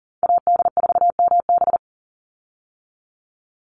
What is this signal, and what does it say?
Morse code, heard as a keyed tone, about 40 wpm, 700 Hz. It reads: AD4MB
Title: USA Call Signs 40wpm